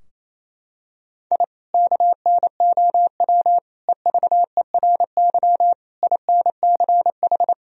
Morse code, heard as a keyed tone, about 28 wpm, 700 Hz. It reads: I KNOW EVERY INCH
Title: Sentences from Top 1000 Words - Review Part 3 - 28wpm